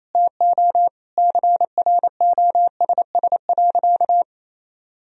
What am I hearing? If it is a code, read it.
Morse code, heard as a keyed tone, about 28 wpm, 700 Hz. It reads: TO CROSS.
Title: Sentences from Top 1000 Words - Review Part 1 - 28wpm